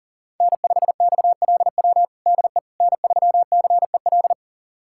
Morse code, heard as a keyed tone, about 40 wpm, 700 Hz. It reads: N5XLW DE N3CEL
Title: USA Callsign DE Callsign 40wpm